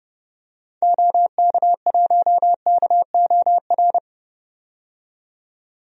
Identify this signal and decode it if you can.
Morse code, heard as a keyed tone, about 30 wpm, 700 Hz. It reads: OK1KOR
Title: DXCC-100 Call Signs - 6 Characters - 30wpm